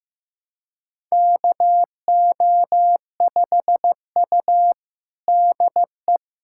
Morse code, heard as a keyed tone, about 15 wpm, 700 Hz. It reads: KO5U DE
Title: USA Callsign DE Callsign - 4 Characters - 15wpm